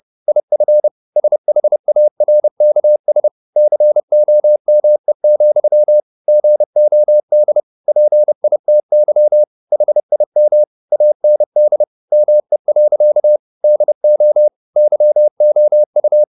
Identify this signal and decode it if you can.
Morse code, heard as a keyed tone, about 30 wpm, 600 Hz. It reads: IF SHARKS COME, GOD PITY HIM AND ME. DO YOU
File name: the old man and the sea / the old man and the sea 1 30wpm